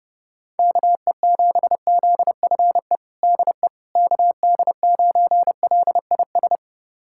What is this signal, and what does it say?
Morse code, heard as a keyed tone, about 32 wpm, 700 Hz. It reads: KE7ZFE DE KD9LIS
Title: USA Callsign DE Callsign 30wpm